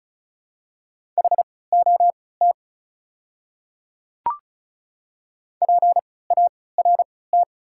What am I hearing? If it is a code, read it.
Morse code, heard as a keyed tone, about 36 wpm, spaced as for 20 wpm, 690 Hz. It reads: HOT PART
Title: 3x Character Spacing - Top 200 Words - 35wpm